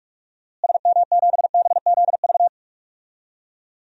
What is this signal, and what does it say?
Morse code, heard as a keyed tone, about 46 wpm, 700 Hz. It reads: SM7BZV
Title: DXCC-100 Call Signs - 6 Characters - Repeated 3x - 45wpm